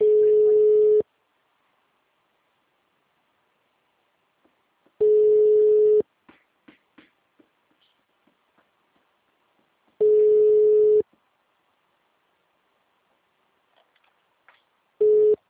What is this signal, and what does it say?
Ringback tone (Europe, 425 Hz), with speech.
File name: out- -2003- - - .234